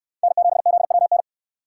Morse code, heard as a keyed tone, about 63 wpm, 700 Hz. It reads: N7ZWN